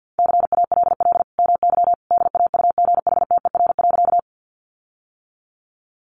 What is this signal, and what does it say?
Morse code, heard as a keyed tone, about 50 wpm, 700 Hz. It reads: CALL MY DAUGHTER.
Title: Sentences from Top 1500 Words 50wpm